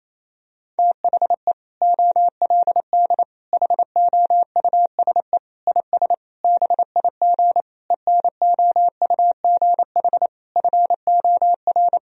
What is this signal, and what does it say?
Morse code, heard as a keyed tone, about 28 wpm, 700 Hz. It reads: THE OLD HOUSE IS BIG ENOUGH FOR
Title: Sentences from Top 1000 Words - Review Part 1 - 28wpm